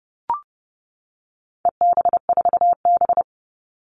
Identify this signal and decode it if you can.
Morse code, heard as a keyed tone, about 30 wpm, 700 Hz. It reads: EB4B
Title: All Cut Numbers - 4-Digit 30wpm